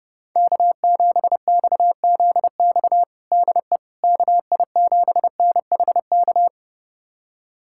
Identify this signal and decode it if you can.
Morse code, heard as a keyed tone, about 31 wpm, 700 Hz. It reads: K7XZX DE KI7NHK
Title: USA Callsign DE Callsign 30wpm